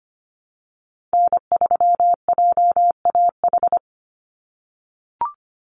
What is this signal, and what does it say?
Morse code, heard as a keyed tone, about 25 wpm, 700 Hz. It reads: N3JAH
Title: USA Call Signs 25wpm